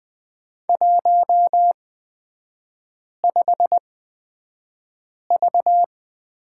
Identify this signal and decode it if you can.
Morse code, heard as a keyed tone, about 20 wpm, spaced as for 5 wpm, 700 Hz. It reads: 15V
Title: 229 Lesson 37e - Letter x - Three Characters 20wpm